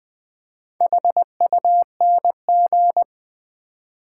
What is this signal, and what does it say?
Morse code, heard as a keyed tone, about 20 wpm, 700 Hz. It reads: HUNG
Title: Build A Word - 2 Letters At A Time - 6-Letter Words - 20wpm